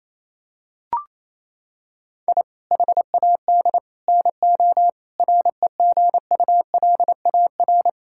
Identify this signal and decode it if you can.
Morse code, heard as a keyed tone, about 27 wpm, 700 Hz. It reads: I HAD NO REGULAR
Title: Sentences from Top 1000 Words - Review Part 6 - 28wpm